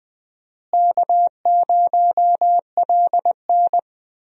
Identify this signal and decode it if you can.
Morse code, heard as a keyed tone, about 20 wpm, 700 Hz. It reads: K0LN
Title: USA Call Signs - 4 Characters - Repeated 3x - 20wpm